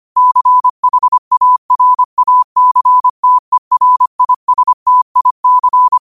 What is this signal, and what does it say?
Morse code, heard as a keyed tone, about 25 wpm, 1000 Hz. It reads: CHARACTERISTIC